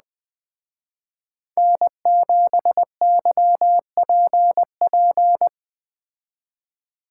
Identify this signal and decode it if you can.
Morse code, heard as a keyed tone, about 20 wpm, 700 Hz. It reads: N7YPP